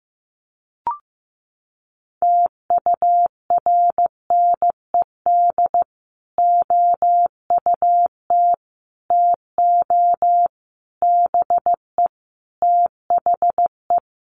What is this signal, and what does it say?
Morse code, heard as a keyed tone, about 15 wpm, 700 Hz. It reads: TURNED OUT TO BE THE
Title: Sets of 5 Words Encore 15wpm